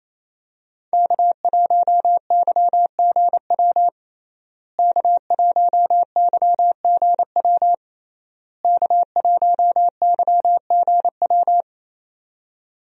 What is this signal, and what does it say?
Morse code, heard as a keyed tone, about 28 wpm, 700 Hz. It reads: K1YGW K1YGW K1YGW
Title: USA Call Signs - 5 Characters - Repeated 3x - 28wpm